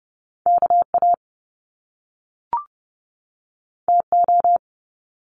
Morse code, heard as a keyed tone, about 30 wpm, 700 Hz. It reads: KA TO
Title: Top Two Letter Combinations in Words 30wpm